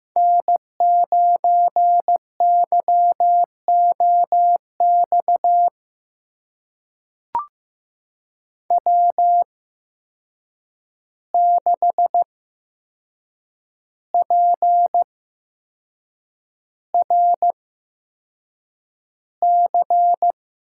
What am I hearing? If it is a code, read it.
Morse code, heard as a keyed tone, about 15 wpm, spaced as for 6 wpm, 700 Hz. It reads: N9YOX W6PRC
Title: USA Call Signs - 5 Characters - 8x Character Spacing - 15wpm